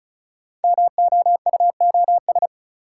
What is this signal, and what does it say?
Morse code, heard as a keyed tone, about 36 wpm, 690 Hz. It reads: MOUOS
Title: Code Groups - 5 Letters - 35wpm